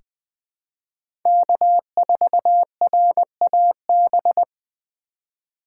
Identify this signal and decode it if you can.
Morse code, heard as a keyed tone, about 20 wpm, 700 Hz. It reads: K4RAB
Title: K1USN SST QSO Practice - 20wpm